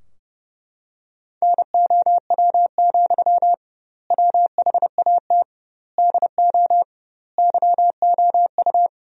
Morse code, heard as a keyed tone, about 32 wpm, 700 Hz. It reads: NOW, WHAT DO YOU
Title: 2x Word-Spacing - Sentences from Top 200 Words - 30wpm